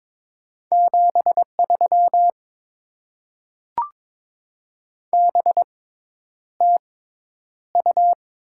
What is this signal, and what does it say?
Morse code, heard as a keyed tone, about 22 wpm, spaced as for 10 wpm, 690 Hz. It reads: 73 BTU
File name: Top 20 QSO Elements - 6x Character Spacing - 22wpm